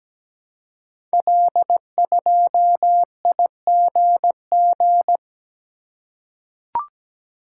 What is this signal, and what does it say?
Morse code, heard as a keyed tone, about 17 wpm, 700 Hz. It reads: L2IGG
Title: Code Groups - 5 Letter:Numbers - 17wpm